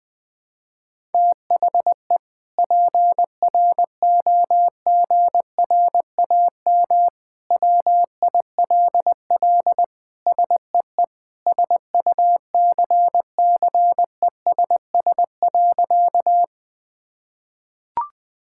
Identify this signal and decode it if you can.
Morse code, heard as a keyed tone, about 21 wpm, 700 Hz. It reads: THE PROGRAM WILL SEE SUCCESS.
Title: Sentences from Top 1000 Words - Review Part 5 - 20wpm